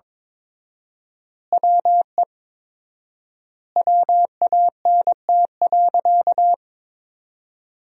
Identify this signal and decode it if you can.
Morse code, heard as a keyed tone, about 23 wpm, 700 Hz. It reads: WE WANT.
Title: 4x Word-Spacing - Sentences from Top 200 Words - 22wpm